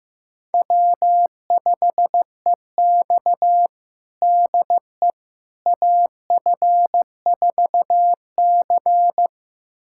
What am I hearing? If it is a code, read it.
Morse code, heard as a keyed tone, about 15 wpm, 700 Hz. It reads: W5EX DE AF4C
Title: USA Callsign DE Callsign - 4 Characters - 15wpm